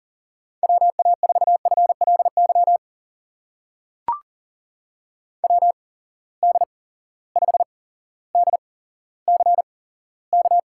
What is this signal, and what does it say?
Morse code, heard as a keyed tone, about 39 wpm, spaced as for 17 wpm, 700 Hz. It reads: WA4FLY WD5DCK
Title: USA Call Signs - 6 Characters - 8x Character Spacing - 40wpm